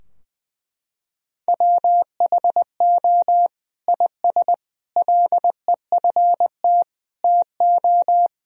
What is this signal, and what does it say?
Morse code, heard as a keyed tone, about 21 wpm, 700 Hz. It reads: WHO IS LEFT TO
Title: Sentences from Top 1000 Words - Review Part 2 - 20wpm